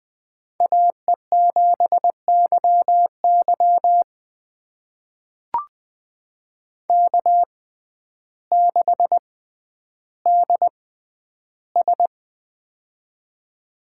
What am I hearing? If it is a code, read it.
Morse code, heard as a keyed tone, about 20 wpm, spaced as for 11 wpm, 700 Hz. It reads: AE7YY K6DS
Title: USA Call Signs - 6x Character Spacing - 20wpm